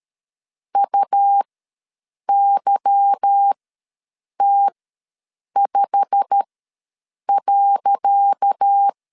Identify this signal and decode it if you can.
Morse code, heard as a keyed tone, about 12 wpm, spaced as for 7 wpm, 800 Hz. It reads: UYT5.
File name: LEZIONI / 10 SIGNS / 10C DOT ws 12wpm 3548